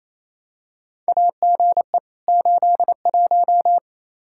Morse code, heard as a keyed tone, about 28 wpm, 700 Hz. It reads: AGE 81
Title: Practice QSO - Exchange 3 - More Info - 28wpm